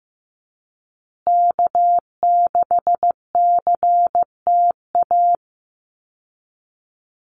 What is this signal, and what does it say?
Morse code, heard as a keyed tone, about 15 wpm, 700 Hz. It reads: K6CTA
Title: Call Signs Repeated 2x - 15wpm